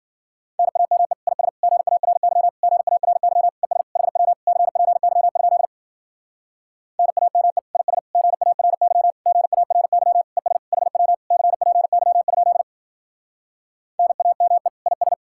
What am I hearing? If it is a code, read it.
Morse code, heard as a keyed tone, about 52 wpm, 700 Hz. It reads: NAME IS GARY GARY ES HW CPY? NAME IS GARY GARY ES HW CPY? NAME IS